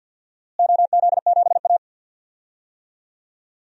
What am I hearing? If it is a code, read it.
Morse code, heard as a keyed tone, about 53 wpm, 700 Hz. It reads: OZ7A
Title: DXCC-100 Call Signs - 4 Characters - Repeated 3x - 50wpm